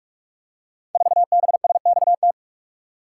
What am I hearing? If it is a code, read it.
Morse code, heard as a keyed tone, about 44 wpm, 700 Hz. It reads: 4BSXT
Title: Code Groups - 5 Letter:Numbers - 45wpm